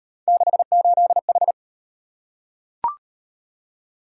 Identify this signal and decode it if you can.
Morse code, heard as a keyed tone, about 38 wpm, 700 Hz. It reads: 68H